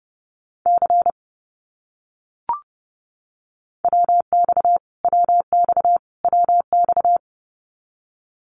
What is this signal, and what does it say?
Morse code, heard as a keyed tone, about 30 wpm, 700 Hz. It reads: C WX WX WX
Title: Abbreviations Repeated 3x - 30wpm